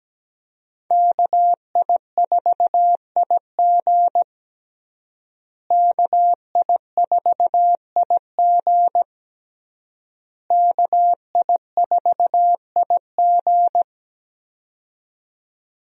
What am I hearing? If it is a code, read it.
Morse code, heard as a keyed tone, about 17 wpm, 700 Hz. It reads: KI4IG KI4IG KI4IG
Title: USA Call Signs - 5 Characters - Repeated 3x - 17wpm